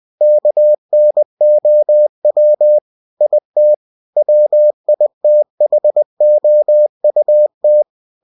Morse code, heard as a keyed tone, about 20 wpm, 600 Hz. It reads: KNOW IT WITHOUT
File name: cw-books / winnie-the-pooh / winnie-20wpm-page-0030